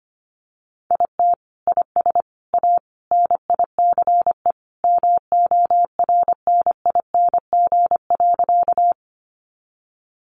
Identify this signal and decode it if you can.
Morse code, heard as a keyed tone, about 25 wpm, 700 Hz. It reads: IT IS A NICE MORNING.
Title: Sentences from Top 1100 Words 25wpm